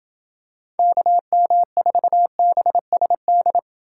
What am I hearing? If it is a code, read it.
Morse code, heard as a keyed tone, about 28 wpm, 700 Hz. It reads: KM4BSD